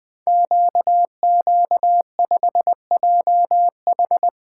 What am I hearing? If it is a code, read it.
Morse code, heard as a keyed tone, about 20 wpm, 700 Hz. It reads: QQ5JH
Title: Code Groups - 5 Letter:Numbers - 20wpm